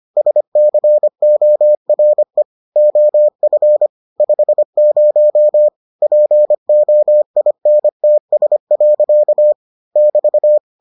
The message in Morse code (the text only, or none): SCORE OF 50 POINTS. <BT>